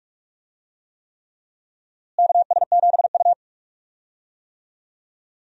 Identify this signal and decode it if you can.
Morse code, heard as a keyed tone, about 43 wpm, 700 Hz. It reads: KS7U
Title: USA Call Signs - 4 Characters - Repeated 3x - 45wpm